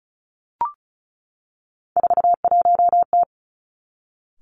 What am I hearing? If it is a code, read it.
Morse code, heard as a keyed tone, about 36 wpm, 700 Hz. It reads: 41T
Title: Three Challenging Letter-Number Combinations 35wpm